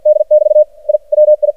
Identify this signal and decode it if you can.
Morse code, heard as a keyed tone, about 56 wpm, 610 Hz. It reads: D <BT> I WI
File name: archived / CRSnet